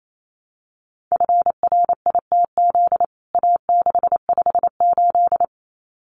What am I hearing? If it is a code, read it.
Morse code, heard as a keyed tone, about 28 wpm, 700 Hz. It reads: FRITZ A658
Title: CWT Exchange 28wpm